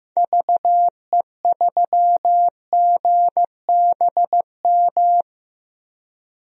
Morse code, heard as a keyed tone, about 15 wpm, 700 Hz. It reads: VE3GBM
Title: DXCC-100 Call Signs - 6 Characters - 15wpm